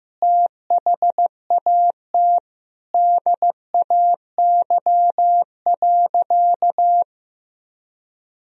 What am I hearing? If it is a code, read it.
Morse code, heard as a keyed tone, about 16 wpm, 700 Hz. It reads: THAT DAY.
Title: Sentences from Top 1000 Words - Review Part 4 - 15wpm